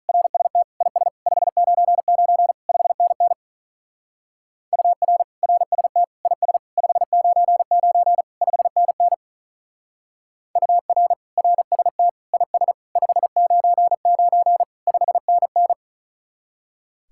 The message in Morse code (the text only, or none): RST IS 599 5NN UR RST IS 599 5NN UR RST IS 599 5NN